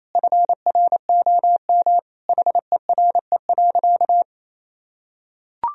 Morse code, heard as a keyed tone, about 28 wpm, 700 Hz. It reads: FROM HERE.
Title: Sentences from Top 1000 Words - Review Part 3 - 28wpm